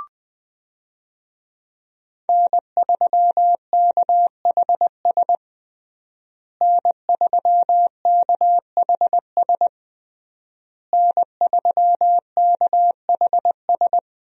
Morse code, heard as a keyed tone, about 21 wpm, 700 Hz. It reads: N3KHS N3KHS N3KHS
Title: USA Call Signs - 5 Characters - Repeated 3x - 20wpm